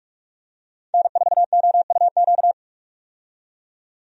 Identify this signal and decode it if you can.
Morse code, heard as a keyed tone, about 46 wpm, 700 Hz. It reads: N4OUQ
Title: USA Call Signs - 5 Characters - 45wpm